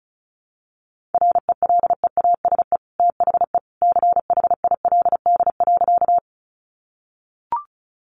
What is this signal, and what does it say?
Morse code, heard as a keyed tone, about 34 wpm, 700 Hz. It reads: RELEASE THE CHILD.
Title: Sentences from Top 1100 Words 35wpm